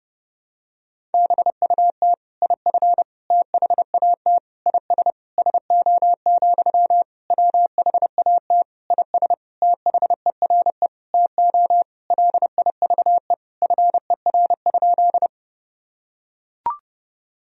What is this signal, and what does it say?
Morse code, heard as a keyed tone, about 30 wpm, 700 Hz. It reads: BUT IF THAT IS SO, WHAT IS THERE TO LIVE FER?
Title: Sentences from Top 200 Words and Abbreviations 30wpm